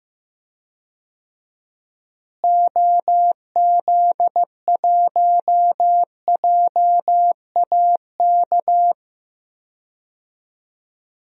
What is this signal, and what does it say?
Morse code, heard as a keyed tone, about 15 wpm, 700 Hz. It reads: OZ1JAK
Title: DXCC-100 Call Signs - 6 Characters - Repeated 2x - 15wpm